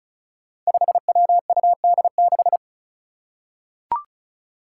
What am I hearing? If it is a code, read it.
Morse code, heard as a keyed tone, about 34 wpm, 700 Hz. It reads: 5WUD6
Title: Five Challenging Letter-Number Combinations 35wpm